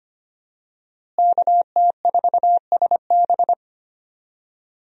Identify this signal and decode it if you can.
Morse code, heard as a keyed tone, about 25 wpm, 700 Hz. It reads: KT4SB
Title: USA Call Signs - 5 Characters - Repeated 3x - 25wpm